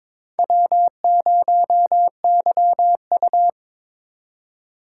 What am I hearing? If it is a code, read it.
Morse code, heard as a keyed tone, about 22 wpm, 700 Hz. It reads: W0YU
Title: USA Call Signs - 4 Characters - 2x Character Spacing - 22wpm